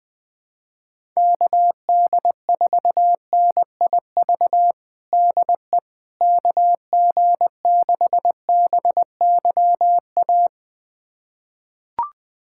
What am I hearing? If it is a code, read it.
Morse code, heard as a keyed tone, about 20 wpm, 700 Hz. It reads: KD4NIV DE KG6BYA
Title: USA Callsign DE Callsign 20wpm